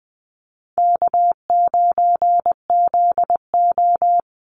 Morse code, heard as a keyed tone, about 19 wpm, 700 Hz. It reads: K9ZO
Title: Easy Call Signs Repeated 3x - 20wpm